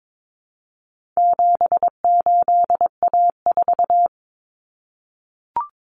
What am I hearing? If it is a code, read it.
Morse code, heard as a keyed tone, about 22 wpm, 700 Hz. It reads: 78A4
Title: Common Cut Numbers - 4-Digit 22wpm